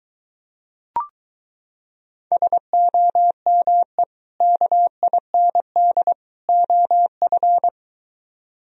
Morse code, heard as a keyed tone, about 23 wpm, 700 Hz. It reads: SOME KIND OF